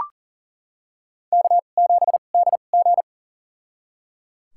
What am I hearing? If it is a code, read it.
Morse code, heard as a keyed tone, about 37 wpm, spaced as for 28 wpm, 700 Hz. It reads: K7DG
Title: USA Call Signs - 4 Characters - 2x Character Spacing - 40wpm